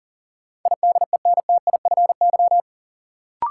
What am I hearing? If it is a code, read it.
Morse code, heard as a keyed tone, about 40 wpm, 700 Hz. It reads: IDENTIFY
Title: Sentences from Top 900 Words - New Word Drill - 40wpm